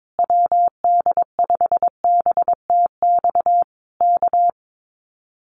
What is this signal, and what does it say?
Morse code, heard as a keyed tone, about 22 wpm, 700 Hz. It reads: WD5BTX K
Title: Common QSO Phrases Encore 22wpm